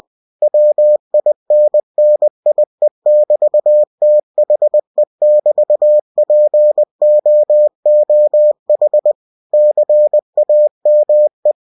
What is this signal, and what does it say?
Morse code, heard as a keyed tone, about 20 wpm, 600 Hz. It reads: WINNIE<BT>THE<BT>POOH CAME
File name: cw-books / winnie-the-pooh / winnie-20wpm-page-0032